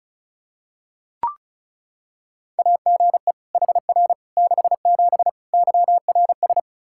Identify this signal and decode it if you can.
Morse code, heard as a keyed tone, about 36 wpm, 700 Hz. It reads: AGE HR 67 YRS
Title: Practice QSO - Exchange 3 - More Info - 35wpm